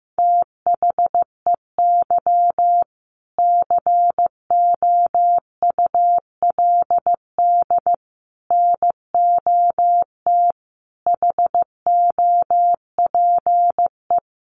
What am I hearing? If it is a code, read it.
Morse code, heard as a keyed tone, about 15 wpm, 700 Hz. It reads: THEY COULD NOT HOPE
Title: Sentences from Top 1100 Words 15wpm